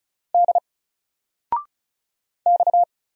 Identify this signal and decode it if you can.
Morse code, heard as a keyed tone, about 35 wpm, 700 Hz. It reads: D X
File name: Single-Letter-Number-35wpm